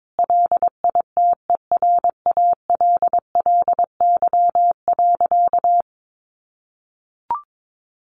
Morse code, heard as a keyed tone, about 22 wpm, 700 Hz. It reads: LITERALLY.
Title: Sentences from Top 1400 Words 22wpm